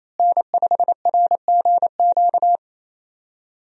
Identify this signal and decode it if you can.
Morse code, heard as a keyed tone, about 28 wpm, 700 Hz. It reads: N5RGQ
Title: USA Call Signs - 5 Characters - Repeated 3x - 28wpm